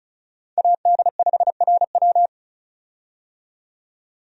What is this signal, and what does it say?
Morse code, heard as a keyed tone, about 36 wpm, 700 Hz. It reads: AD5RW
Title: USA Call Signs - 5 Characters - 35wpm